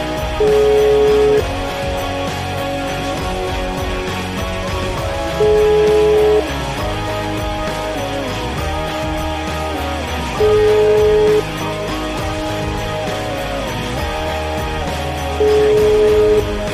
Ringback tone (Europe, 425 Hz), with noise.